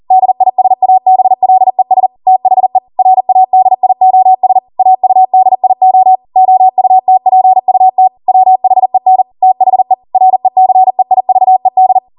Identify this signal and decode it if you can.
Morse code, heard as a keyed tone, about 40 wpm, 750 Hz. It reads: DISABLES THE RADIOS AUDIO OUTPUT WHEN THE RECEIVED
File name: Morse / 40WPM